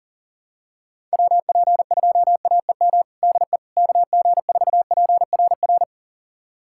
Morse code, heard as a keyed tone, about 40 wpm, 700 Hz. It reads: WP2AEM DE KG4PRR
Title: USA Callsign DE Callsign - 6 Characters - 40wpm